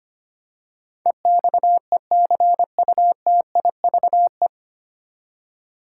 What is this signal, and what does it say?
Morse code, heard as a keyed tone, about 25 wpm, 700 Hz. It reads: EXECUTIVE
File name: Nine Letter Words - 25wpm